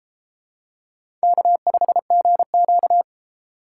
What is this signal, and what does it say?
Morse code, heard as a keyed tone, about 29 wpm, 700 Hz. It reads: K5GQ